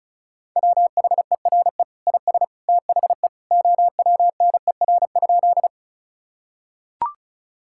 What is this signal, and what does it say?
Morse code, heard as a keyed tone, about 36 wpm, 700 Hz. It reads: WHERE IS THE OWNER?
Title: Sentences from Top 1600 Words 35wpm